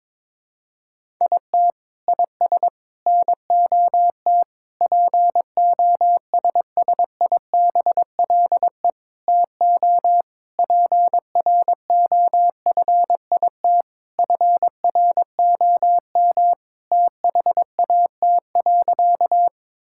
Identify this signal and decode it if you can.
Morse code, heard as a keyed tone, about 22 wpm, 700 Hz. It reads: IT IS NOT POSSIBLE TO PROFIT FROM THAT.
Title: Sentences from Top 1000 Words - Review Part 5 - 22wpm